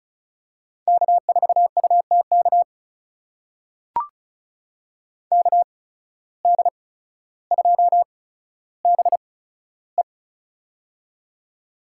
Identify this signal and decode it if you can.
Morse code, heard as a keyed tone, about 36 wpm, spaced as for 15 wpm, 700 Hz. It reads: K4UTK KD2BE
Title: USA Call Signs - 5 Characters - 8x Character Spacing - 35wpm